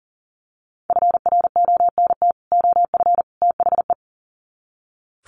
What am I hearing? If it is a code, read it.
Morse code, heard as a keyed tone, about 40 wpm, 700 Hz. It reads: FRONT OF THE
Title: Sets of 3 Words 40wpm